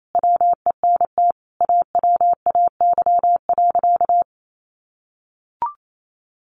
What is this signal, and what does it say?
Morse code, heard as a keyed tone, about 28 wpm, 700 Hz. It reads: WENT AWAY.
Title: Sentences from Top 600 Words 28wpm